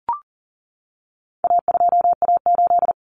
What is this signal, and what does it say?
Morse code, heard as a keyed tone, about 40 wpm, 700 Hz. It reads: A2A8
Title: Common Cut Numbers - 4-Digit 40wpm